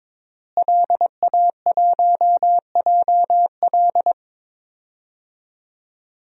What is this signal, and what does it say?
Morse code, heard as a keyed tone, about 21 wpm, 700 Hz. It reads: LA1JL
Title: DXCC-100 Call Signs - 5 Characters - Repeated 2x - 22wpm